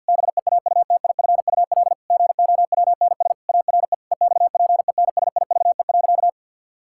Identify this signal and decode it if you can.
Morse code, heard as a keyed tone, about 50 wpm, 700 Hz. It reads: BEAUTIFUL GOWNS ARE EXPENSIVE.
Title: Pangram Challenge v2 - 50wpm